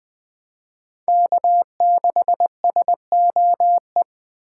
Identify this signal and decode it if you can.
Morse code, heard as a keyed tone, about 21 wpm, 700 Hz. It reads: K6SOE
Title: Speed Racing - USA 1x3 Call Signs 17wpm